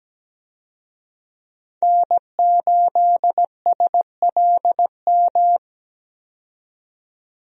Morse code, heard as a keyed tone, about 17 wpm, 700 Hz. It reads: N8SLM
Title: USA Call Signs - 5 Characters - 17wpm